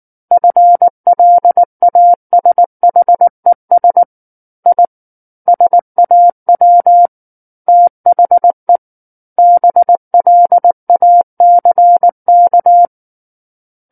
Morse code, heard as a keyed tone, about 19 wpm, 700 Hz. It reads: FLASHES I SAW THE BLACK
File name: Book One / 10-Chapter 10 - 19 WPM